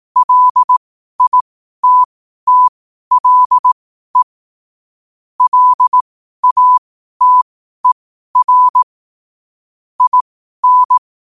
Morse code, heard as a keyed tone, about 18 wpm, spaced as for 12 wpm, 1000 Hz. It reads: LITTLE LATER IN
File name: qotdfiles / 12wpm